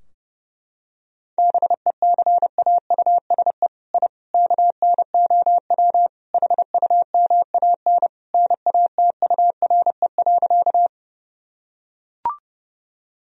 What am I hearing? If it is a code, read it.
Morse code, heard as a keyed tone, about 30 wpm, 700 Hz. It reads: BECAUSE I KNOW HUMAN NATURE.
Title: Sentences from Top 1000 Words - Review Part 3 - 30wpm